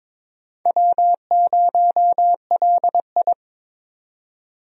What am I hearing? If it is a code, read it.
Morse code, heard as a keyed tone, about 22 wpm, 700 Hz. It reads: W0LI
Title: USA Call Signs - 4 Characters - 22wpm